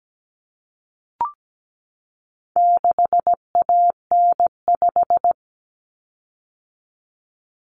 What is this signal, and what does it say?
Morse code, heard as a keyed tone, about 17 wpm, 700 Hz. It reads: 6AN5
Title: Common Cut Numbers - 4-Digit 17wpm